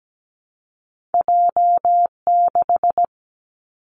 Morse code, heard as a keyed tone, about 17 wpm, 700 Hz. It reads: J6
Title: Two Challenging Letter-Number Combinations 17wpm